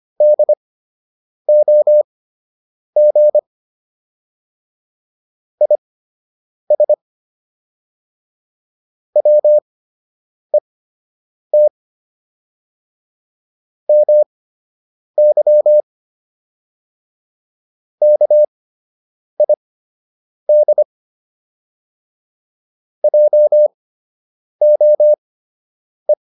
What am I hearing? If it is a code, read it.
Morse code, heard as a keyed tone, about 25 wpm, spaced as for 8 wpm, 600 Hz. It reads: DOG IS WET MY KID JOE